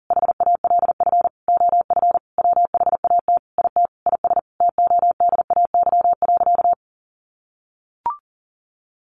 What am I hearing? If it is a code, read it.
Morse code, heard as a keyed tone, about 40 wpm, 700 Hz. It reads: HALF OF WHAT IT IS TODAY.
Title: Sentences from Top 1500 Words 40wpm